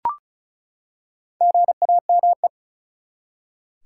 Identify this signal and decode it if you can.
Morse code, heard as a keyed tone, about 36 wpm, 700 Hz. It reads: GAME
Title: Four Letter Words - 35wpm